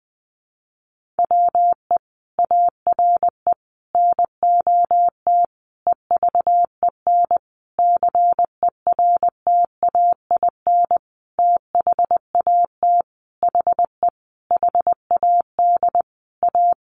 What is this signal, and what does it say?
Morse code, heard as a keyed tone, about 20 wpm, 700 Hz. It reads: WE ARE NOT EVEN CERTAIN THAT HE HAD A
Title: Sentences from Top 1500 Words 20wpm